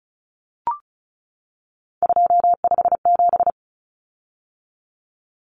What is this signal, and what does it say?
Morse code, heard as a keyed tone, about 34 wpm, 700 Hz. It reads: 257
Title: Common Cut Numbers - 3-Digit 35wpm